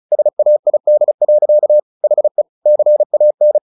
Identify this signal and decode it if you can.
Morse code, heard as a keyed tone, about 34 wpm, 600 Hz. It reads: SAID. HE CAN
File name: the old man and the sea 2 35wpm